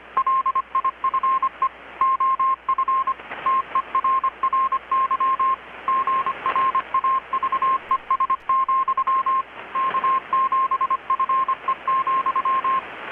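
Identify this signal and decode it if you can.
Morse code, heard as a keyed tone, about 25 wpm, 1060 Hz. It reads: LIFE OF TERRY GRAVES, K7FE,
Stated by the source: Morse from KPH's Night of Nights XX maritime radiotelegraphy event, received in Chiapas, Mexico about 3,800 km away on a Kenwood R-600 with a grounded 10 m long-wire antenna.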